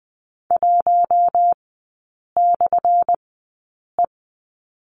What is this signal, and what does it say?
Morse code, heard as a keyed tone, about 20 wpm, spaced as for 8 wpm, 700 Hz. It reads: 1/E
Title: 178 Lesson 29e - Slash - Three Characters 20wpm